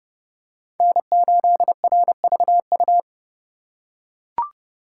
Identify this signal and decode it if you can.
Morse code, heard as a keyed tone, about 29 wpm, 700 Hz. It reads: N8RVU